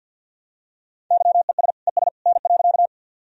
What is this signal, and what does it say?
Morse code, heard as a keyed tone, about 51 wpm, 700 Hz. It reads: YES ES N.
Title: Sentences from Top 500 Words and Abbreviations 50wpm